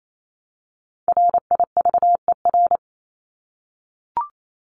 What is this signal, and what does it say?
Morse code, heard as a keyed tone, about 28 wpm, 700 Hz. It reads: RIVER
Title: Top 500 Words 28wpm